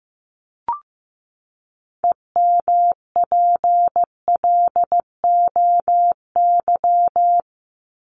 Morse code, heard as a keyed tone, about 15 wpm, 700 Hz. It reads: EMPLOY
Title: ED Suffix - Top 100 Words with Base Words 15wpm